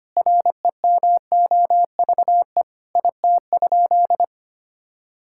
Morse code, heard as a keyed tone, about 25 wpm, 700 Hz. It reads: REMOVE IT?
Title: Sentences from Top 1000 Words - Review Part 5 - 25wpm